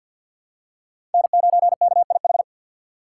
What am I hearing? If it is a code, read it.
Morse code, heard as a keyed tone, about 48 wpm, 700 Hz. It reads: N9KIH
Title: USA Call Signs - 5 Characters - 3x Character Spacing - 50wpm